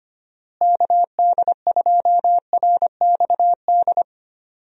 Morse code, heard as a keyed tone, about 25 wpm, 700 Hz. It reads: KD2RXD
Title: USA Call Signs - 6 Characters - Repeated 2x - 25wpm